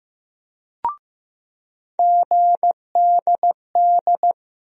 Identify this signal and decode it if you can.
Morse code, heard as a keyed tone, about 15 wpm, 700 Hz. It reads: GDD
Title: USA Call Sign Postfixes - 3 Letters 15wpm